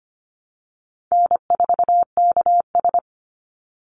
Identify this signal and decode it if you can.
Morse code, heard as a keyed tone, about 25 wpm, 700 Hz. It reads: N4KS
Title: Easy Call Signs Repeated 2x - 25wpm